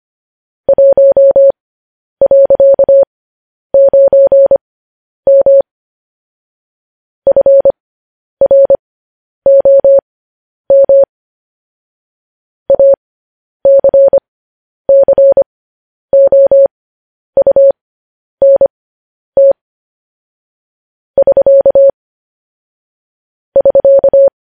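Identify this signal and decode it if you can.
Morse code, heard as a keyed tone, about 25 wpm, spaced as for 10 wpm, 550 Hz. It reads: 1.9M FROM ACCOUNT <SK> <SK>
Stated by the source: Morse code news headlines broadcast.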